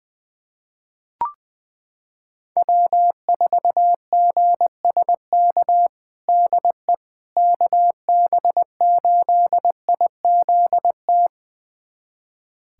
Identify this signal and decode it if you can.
Morse code, heard as a keyed tone, about 21 wpm, 700 Hz. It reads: W4GSK DE KB8IZT
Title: USA Callsign DE Callsign 20wpm